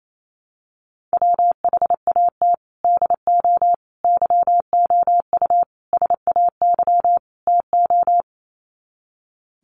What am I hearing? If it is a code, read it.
Morse code, heard as a keyed tone, about 28 wpm, 700 Hz. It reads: WHAT DO YOU SAY TO
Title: Sets of 5 Words 28wpm